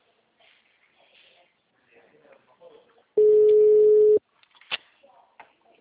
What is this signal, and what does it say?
Ringback tone (Europe, 425 Hz), with speech.